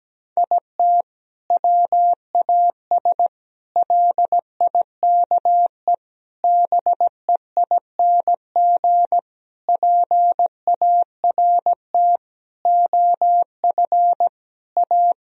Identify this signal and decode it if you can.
Morse code, heard as a keyed tone, about 17 wpm, 700 Hz. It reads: IT WAS LIKE BEING PART OF A
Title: Sentences from Top 1000 Words - Review Part 3 - 17wpm